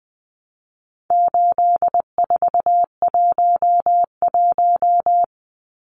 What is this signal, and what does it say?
Morse code, heard as a keyed tone, about 20 wpm, 700 Hz. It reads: 8411
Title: Numbers - 4-Digit 20wpm